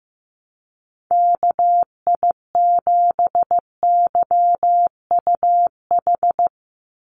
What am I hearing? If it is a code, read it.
Morse code, heard as a keyed tone, about 15 wpm, 700 Hz. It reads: KI7YUH
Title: USA Call Signs 15wpm